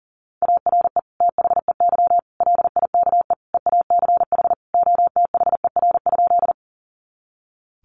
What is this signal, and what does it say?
Morse code, heard as a keyed tone, about 37 wpm, 700 Hz. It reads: ARE THEY LIKE EACH OTHER?
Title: Sentences from Top 200 Words 40wpm